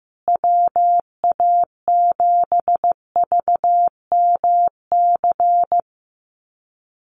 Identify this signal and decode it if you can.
Morse code, heard as a keyed tone, about 15 wpm, 700 Hz. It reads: WA7VMC
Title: USA Call Signs Repeated 3x 15wpm